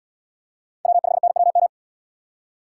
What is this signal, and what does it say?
Morse code, heard as a keyed tone, about 77 wpm, 700 Hz. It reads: W5NWM